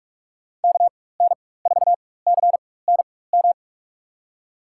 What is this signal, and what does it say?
Morse code, heard as a keyed tone, about 47 wpm, spaced as for 21 wpm, 700 Hz. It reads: KN4CNM
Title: USA Call Signs - 4x Character Spacing - 45wpm